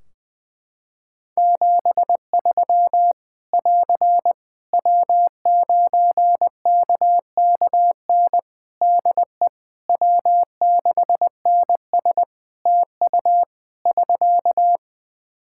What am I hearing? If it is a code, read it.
Morse code, heard as a keyed tone, about 20 wpm, 700 Hz. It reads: 73 <AR> W9KKN DE W6NS TU <SK>
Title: Practice QSO - Exchange 4 - Ending - 20wpm